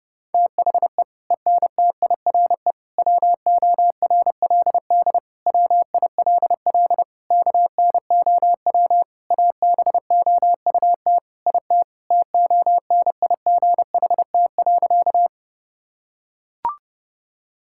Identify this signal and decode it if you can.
Morse code, heard as a keyed tone, about 30 wpm, 700 Hz. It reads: THE ENTIRE WORLD WILL KNOW ABOUT IT TONIGHT.
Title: Sentences from Top 1000 Words - Review Part 5 - 30wpm